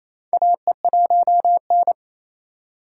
Morse code, heard as a keyed tone, about 28 wpm, 700 Hz. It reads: AE1N
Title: K1USN SST QSO Practice - 28wpm